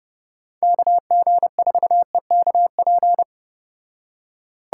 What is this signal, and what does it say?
Morse code, heard as a keyed tone, about 30 wpm, 700 Hz. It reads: KG4EKP